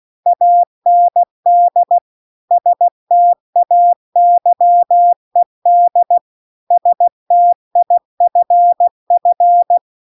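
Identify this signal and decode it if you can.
Morse code, heard as a keyed tone, about 16 wpm, 700 Hz. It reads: AND STAYED STIFF
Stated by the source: Morse practice text at 16 WPM.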